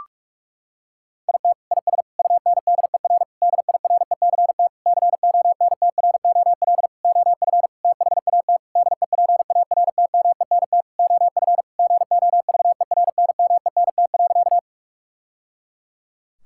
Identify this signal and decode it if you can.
Morse code, heard as a keyed tone, about 46 wpm, 700 Hz. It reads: IT IS UNDER DIRECT CONTROL OF THAT DEPARTMENT OF GOVERNMENT.
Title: Sentences from Top 1000 Words - Review Part 5 - 45wpm